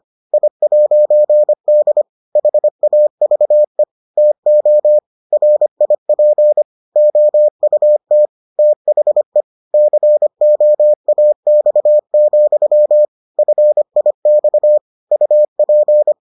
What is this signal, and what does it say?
Morse code, heard as a keyed tone, about 25 wpm, 600 Hz. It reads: I'D HAVE TO RIP OUT THE COAX, FIX UP